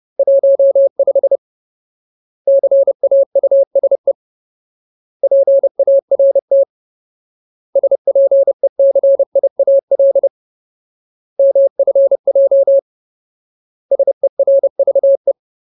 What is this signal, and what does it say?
Morse code, heard as a keyed tone, about 30 wpm, 550 Hz. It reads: 15 CAUSE PART SPECIAL MFJ SERVE
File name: Ultimate Word List / 30WPM Ultimate